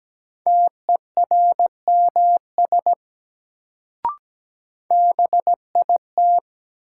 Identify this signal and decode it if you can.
Morse code, heard as a keyed tone, about 17 wpm, 700 Hz. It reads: TERMS BIT
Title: Sets of 3 Words - New Word Drill - 17wpm